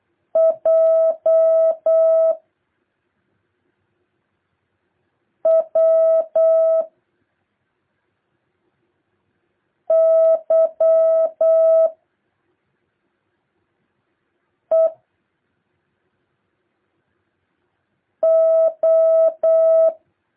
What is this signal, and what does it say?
Morse code, heard as a keyed tone, about 8 wpm, 650 Hz. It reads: J W Y E O
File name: jh8cbh / 2023cw